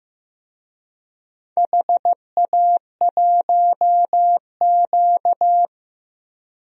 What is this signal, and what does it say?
Morse code, heard as a keyed tone, about 15 wpm, 700 Hz. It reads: HA1Q
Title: DXCC-100 Call Signs - 4 Characters - Repeated 3x - 15wpm